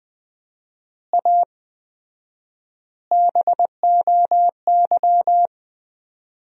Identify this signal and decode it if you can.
Morse code, heard as a keyed tone, about 20 wpm, 700 Hz. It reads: A BOY
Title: 4x Word-Spacing - Sentences from Top 400 Words - 20wpm